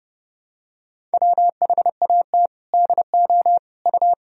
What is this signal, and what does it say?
Morse code, heard as a keyed tone, about 30 wpm, 700 Hz. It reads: WHAT DO U
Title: Sentences from Top 200 Words and Abbreviations 30wpm